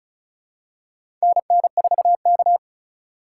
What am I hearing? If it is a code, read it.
Morse code, heard as a keyed tone, about 36 wpm, 700 Hz. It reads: NN4K
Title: USA Call Signs - 4 Characters - 6x Character Spacing - 35wpm